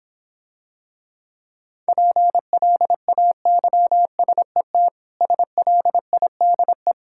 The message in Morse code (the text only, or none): PLAYSET SLIDE